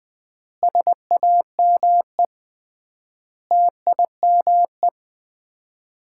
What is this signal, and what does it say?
Morse code, heard as a keyed tone, about 20 wpm, 700 Hz. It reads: SAME TIME
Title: 3x Word-Spacing - Sets of 5 Words - 20wpm